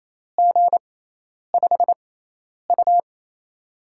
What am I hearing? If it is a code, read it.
Morse code, heard as a keyed tone, about 28 wpm, spaced as for 10 wpm, 700 Hz. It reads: G5U
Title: USA Call Signs - 6x Character Spacing - 28wpm